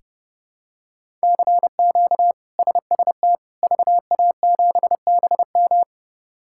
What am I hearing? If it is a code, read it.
Morse code, heard as a keyed tone, about 32 wpm, 700 Hz. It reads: CQ SST VA7BM
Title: K1USN SST QSO Practice - 30wpm